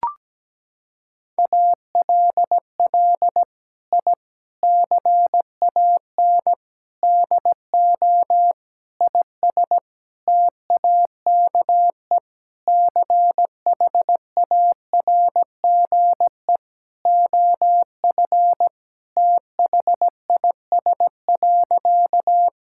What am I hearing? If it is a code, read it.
Morse code, heard as a keyed tone, about 17 wpm, 700 Hz. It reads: ALL I CAN DO IS TAKE CHARGE OF THIS.
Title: Sentences from Top 1000 Words - Review Part 6 - 17wpm